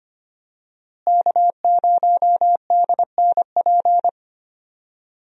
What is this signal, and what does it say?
Morse code, heard as a keyed tone, about 25 wpm, 700 Hz. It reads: K0DNP
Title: USA Call Signs - 5 Characters - Repeated 2x - 25wpm